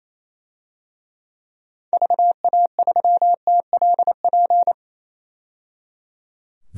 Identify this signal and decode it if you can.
Morse code, heard as a keyed tone, about 28 wpm, 700 Hz. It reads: VA3TLP
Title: DXCC-100 Call Signs - 6 Characters - 28wpm